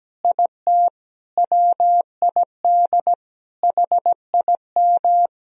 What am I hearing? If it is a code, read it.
Morse code, heard as a keyed tone, about 17 wpm, 700 Hz. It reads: IT WID HIM
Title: Sentences from Top 500 Words and Abbreviations 17wpm